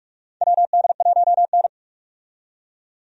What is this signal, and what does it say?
Morse code, heard as a keyed tone, about 41 wpm, 700 Hz. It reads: WD1N
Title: USA Call Signs - 4 Characters - 45wpm